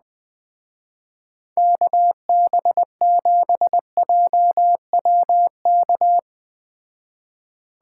Morse code, heard as a keyed tone, about 20 wpm, 700 Hz. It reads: KB7JWK
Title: Speed Racing - USA 2x3 Call Signs 17wpm